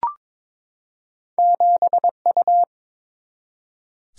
Morse code, heard as a keyed tone, about 22 wpm, 700 Hz. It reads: 7U
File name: practice / 22wpm / Two Letter-Number Combinations - 22wpm